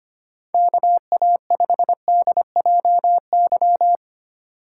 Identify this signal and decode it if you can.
Morse code, heard as a keyed tone, about 25 wpm, 700 Hz. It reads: KA5DJY
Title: USA Call Signs - 6 Characters - 6x Character Spacing - 25wpm